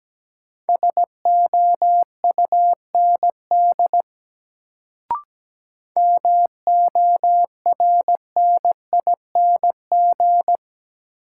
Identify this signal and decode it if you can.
Morse code, heard as a keyed tone, about 17 wpm, 700 Hz. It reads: SOUND MORNING
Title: Sets of 3 Words - New Word Drill - 17wpm